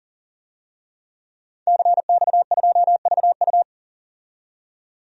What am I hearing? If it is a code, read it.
Morse code, heard as a keyed tone, about 40 wpm, 700 Hz. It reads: CX2VU
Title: DXCC-100 Call Signs - 5 Characters - 40wpm